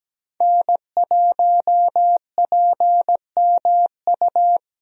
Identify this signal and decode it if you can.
Morse code, heard as a keyed tone, about 17 wpm, 700 Hz. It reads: N1PMU
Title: USA Call Signs - 5 Characters - 17wpm